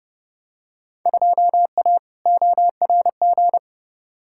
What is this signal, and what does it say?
Morse code, heard as a keyed tone, about 30 wpm, 700 Hz. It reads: 2A ORG
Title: Field Day Exchanges Encore - 30wpm